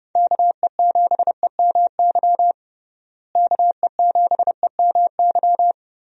Morse code, heard as a keyed tone, about 30 wpm, 700 Hz. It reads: KE7EMY KE7EMY
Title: USA Call Signs - 6 Characters - Repeated 3x - 30wpm